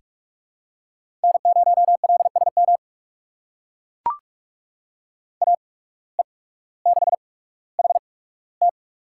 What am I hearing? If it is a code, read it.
Morse code, heard as a keyed tone, about 44 wpm, spaced as for 19 wpm, 700 Hz. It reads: N0LSM AE6HT
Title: USA Call Signs - 5 Characters - 8x Character Spacing - 45wpm